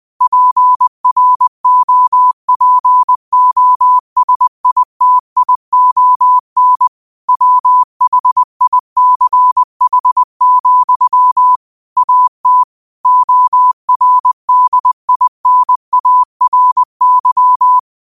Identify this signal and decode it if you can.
Morse code, heard as a keyed tone, about 20 wpm, 1000 Hz. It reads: PROPOSITION WHICH, AT ORDINARY